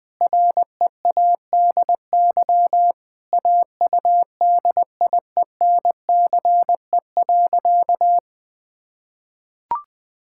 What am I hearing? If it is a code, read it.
Morse code, heard as a keyed tone, about 19 wpm, 700 Hz. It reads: READY AUDIENCE.
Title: Sentences from Top 1000 Words - Review Part 5 - 20wpm